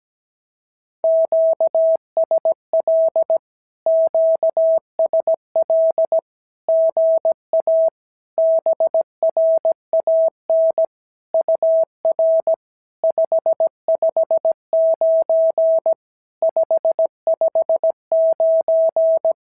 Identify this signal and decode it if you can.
Morse code, heard as a keyed tone, about 17 wpm, 650 Hz. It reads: QSL QSL GA BRAN UR 559 559
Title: SOTA QSO Practice 17wpm